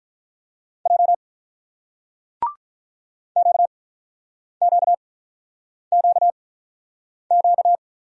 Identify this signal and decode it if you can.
Morse code, heard as a keyed tone, about 46 wpm, spaced as for 9 wpm, 700 Hz. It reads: J QQQQ